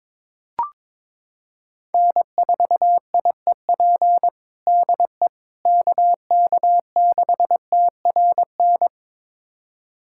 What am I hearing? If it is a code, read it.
Morse code, heard as a keyed tone, about 22 wpm, 700 Hz. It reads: N4IEP DE KK6TRN
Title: USA Callsign DE Callsign 22wpm